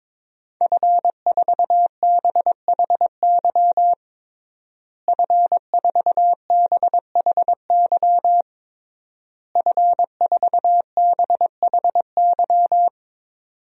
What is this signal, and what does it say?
Morse code, heard as a keyed tone, about 21 wpm, 700 Hz. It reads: F4BHY F4BHY F4BHY
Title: DXCC-100 Call Signs - 5 Characters - Repeated 3x - 22wpm